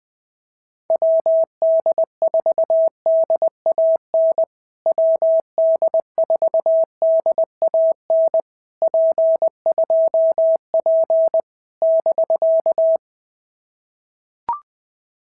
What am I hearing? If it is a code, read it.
Morse code, heard as a keyed tone, about 20 wpm, 650 Hz. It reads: WD4DAN WD4DAN P2P <BK>
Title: POTA QSO Practice - 20wpm